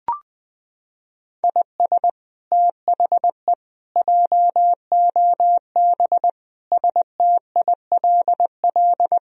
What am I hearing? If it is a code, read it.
Morse code, heard as a keyed tone, about 20 wpm, 700 Hz. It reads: IS THE JOB STILL
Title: Sentences from Top 1000 Words - Review Part 3 - 20wpm